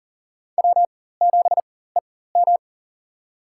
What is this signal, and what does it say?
Morse code, heard as a keyed tone, about 40 wpm, spaced as for 19 wpm, 700 Hz. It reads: W7EM
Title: USA Call Signs - 4 Characters - 4x Character Spacing - 40wpm